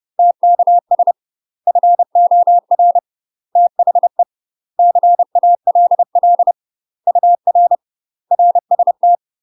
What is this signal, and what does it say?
Morse code, heard as a keyed tone, about 30 wpm, 700 Hz. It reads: TKS FOR THE CALL UR RST
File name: UK MASTER QSOs TRAINING / 30WPM / UK MASTER QSOs 0023